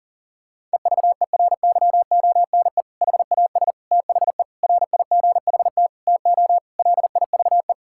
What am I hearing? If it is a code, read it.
Morse code, heard as a keyed tone, about 40 wpm, 700 Hz. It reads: EVERYONE HAS THE RIGHT TO LIVE
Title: Sentences from Top 1000 Words - Review Part 5 - 40wpm